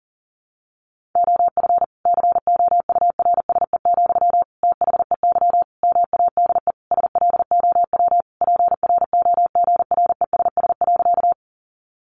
Morse code, heard as a keyed tone, about 40 wpm, 700 Hz. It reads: OF COURSE, THEY MADE SLOW PROGRESS.
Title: Sentences from Top 1400 Words 40wpm